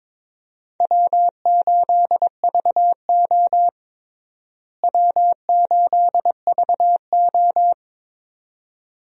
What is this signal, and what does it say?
Morse code, heard as a keyed tone, about 21 wpm, 700 Hz. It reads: W8VO W8VO
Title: USA Call Signs - 4 Characters - Repeated 2x - 22wpm